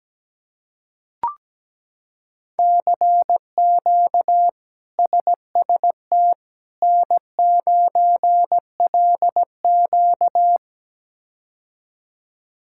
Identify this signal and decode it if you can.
Morse code, heard as a keyed tone, about 17 wpm, 700 Hz. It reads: CQ SST N9LQ
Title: K1USN SST QSO Practice - 17wpm